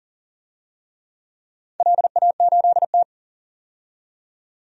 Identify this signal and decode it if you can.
Morse code, heard as a keyed tone, about 40 wpm, 700 Hz. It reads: LA8T
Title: DXCC-100 Call Signs - 4 Characters - 40wpm